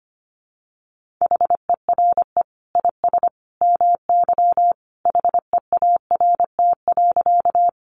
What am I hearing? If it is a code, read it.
Morse code, heard as a keyed tone, about 25 wpm, 700 Hz. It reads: HERE IS MY HEART.
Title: Sentences from Top 1200 Words 25wpm